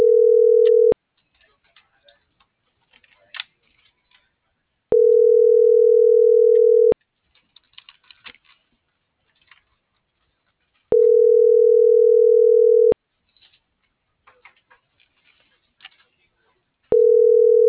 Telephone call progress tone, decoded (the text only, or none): ringback tone